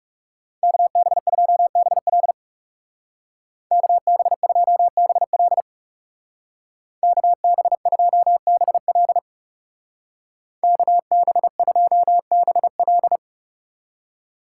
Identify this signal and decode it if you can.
Morse code, heard as a keyed tone, about 38 wpm, 700 Hz. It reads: KB2BL KB2BL KB2BL KB2BL